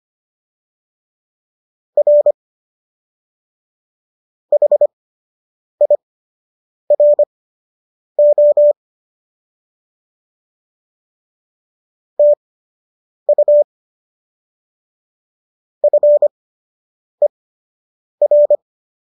Morse code, heard as a keyed tone, about 25 wpm, spaced as for 8 wpm, 590 Hz. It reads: R HIRO TU FER